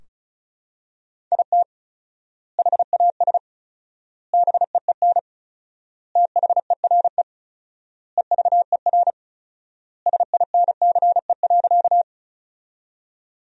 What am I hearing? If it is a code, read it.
Morse code, heard as a keyed tone, about 34 wpm, 700 Hz. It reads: IT HAS BEEN THERE EVER SINCE.
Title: 4x Word-Spacing - Sentences from Top 400 Words - 35wpm